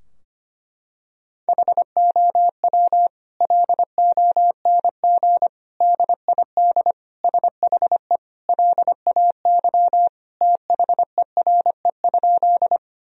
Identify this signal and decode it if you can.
Morse code, heard as a keyed tone, about 25 wpm, 700 Hz. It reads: HOW LONG DID SHE LAY THERE?
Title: Sentences from Top 1000 Words - Review Part 2 - 25wpm